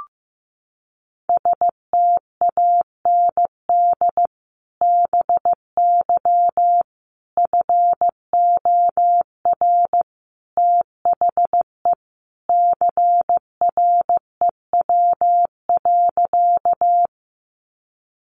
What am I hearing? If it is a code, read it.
Morse code, heard as a keyed tone, about 15 wpm, 700 Hz. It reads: STAND BY FOR THE CREW.
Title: Sentences from Top 1500 Words 15wpm